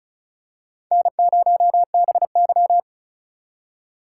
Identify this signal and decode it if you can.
Morse code, heard as a keyed tone, about 34 wpm, 700 Hz. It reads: N0BY
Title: USA Call Signs - 4 Characters - 2x Character Spacing - 35wpm